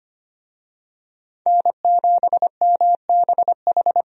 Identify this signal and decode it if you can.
Morse code, heard as a keyed tone, about 25 wpm, 700 Hz. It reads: N7MBH
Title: USA Call Signs - 5 Characters - 25wpm